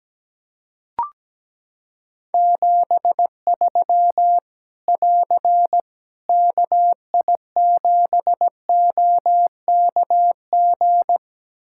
Morse code, heard as a keyed tone, about 17 wpm, 700 Hz. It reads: 73 <AR> KI7OKG
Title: K1USN SST QSO Practice 17wpm